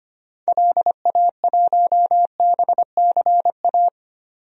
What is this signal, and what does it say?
Morse code, heard as a keyed tone, about 25 wpm, 700 Hz. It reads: LA1BCA
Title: DXCC-100 Call Signs - 6 Characters - 25wpm